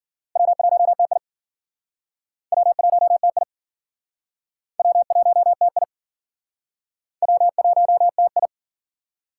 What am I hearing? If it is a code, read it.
Morse code, heard as a keyed tone, about 51 wpm, 700 Hz. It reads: W1TI W1TI W1TI W1TI